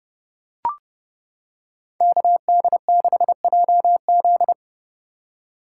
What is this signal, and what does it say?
Morse code, heard as a keyed tone, about 29 wpm, 700 Hz. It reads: KD6JZ